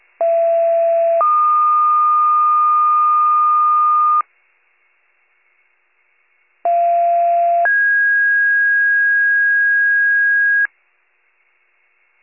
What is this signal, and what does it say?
Two-tone page, heard as a one-second tone, then a three-second tone, about 670 then 1220 Hz, about 690 then 1650 Hz. It is heard from 0.2 to 10.7 s.